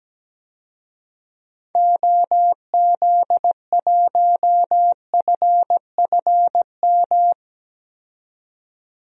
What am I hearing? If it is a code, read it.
Morse code, heard as a keyed tone, about 17 wpm, 700 Hz. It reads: OZ1FFM
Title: DXCC-100 Call Signs - 6 Characters - 17wpm